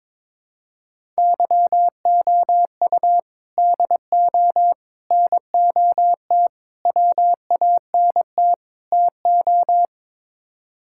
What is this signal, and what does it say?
Morse code, heard as a keyed tone, about 22 wpm, 700 Hz. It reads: YOU DO NOT WANT TO
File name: practice / 22wpm / Build A Phrase - 5-Word Phrases - 22wpm